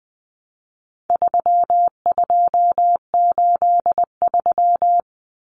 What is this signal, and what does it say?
Morse code, heard as a keyed tone, about 20 wpm, 700 Hz. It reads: 3283
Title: Common Cut Numbers - 4-Digit 20wpm